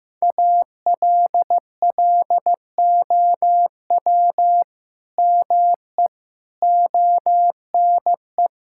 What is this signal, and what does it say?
Morse code, heard as a keyed tone, about 16 wpm, 700 Hz. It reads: ALLOW ME ONE
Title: Sentences from Top 1000 Words - Review Part 5 - 15wpm